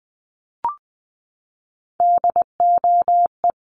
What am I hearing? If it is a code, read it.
Morse code, heard as a keyed tone, about 20 wpm, 700 Hz. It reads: DOE
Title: Sets of 4 Words 20wpm